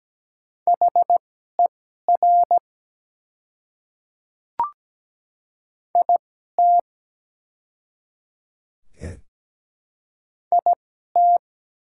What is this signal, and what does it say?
Morse code, heard as a keyed tone, about 17 wpm, spaced as for 12 wpm, 700 Hz. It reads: HER IT IT
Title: 2x Character Spacing - Top 100 Words - 17wpm